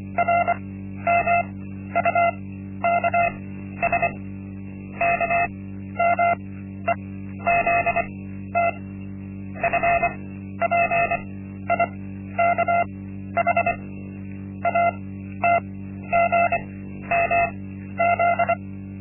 Morse code, heard as a keyed tone, about 24 wpm, spaced as for 12 wpm, 690 Hz. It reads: RMUKS KMEZT FPIKH ATGMZ
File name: Linksmakalnis-1